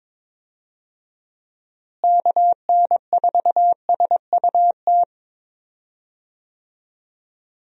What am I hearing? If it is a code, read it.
Morse code, heard as a keyed tone, about 22 wpm, 700 Hz. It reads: KN4SUT
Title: USA Call Signs - 6 Characters - Repeated 2x - 22wpm